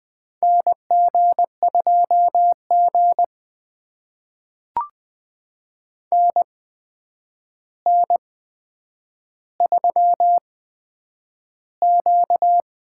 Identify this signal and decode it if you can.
Morse code, heard as a keyed tone, about 20 wpm, spaced as for 9 wpm, 700 Hz. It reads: NG2G NN3Q
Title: USA Call Signs - 4 Characters - 8x Character Spacing - 20wpm